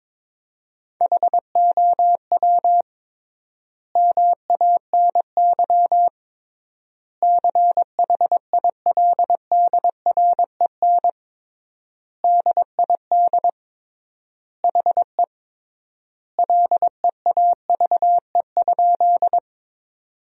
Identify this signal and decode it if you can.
Morse code, heard as a keyed tone, about 22 wpm, 700 Hz. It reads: HOW MANY CHILDREN DID HE LEAVE?
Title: 3x Word-Spacing - Sentences from Top 400 Words - 22wpm